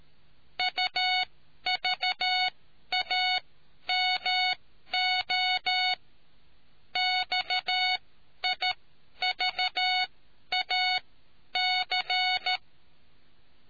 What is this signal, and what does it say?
Morse code, heard as a keyed tone, about 13 wpm, spaced as for 10 wpm, 740 Hz. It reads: UVAMO XIVAC